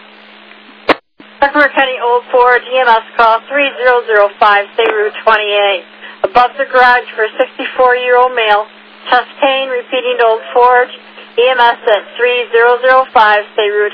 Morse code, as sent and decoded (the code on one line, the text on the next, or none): none